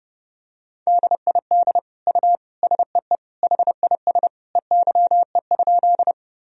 Morse code, heard as a keyed tone, about 29 wpm, 700 Hz. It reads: DID U SEE HIS EYE?
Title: Sentences from Top 300 Words and Abbreviations 30wpm